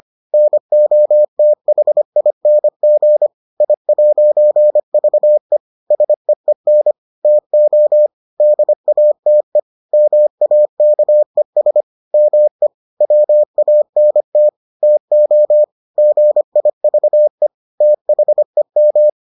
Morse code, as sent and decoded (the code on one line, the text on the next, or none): -. --- - .... .. -. --. / .. .----. ...- . / ... . -. / - --- / -.. .- - . / -- .- -.- . ... / -- . / .-- .- -. - / - --- / --. .. ...- . / - .... --
NOTHING I'VE SEEN TO DATE MAKES ME WANT TO GIVE THEM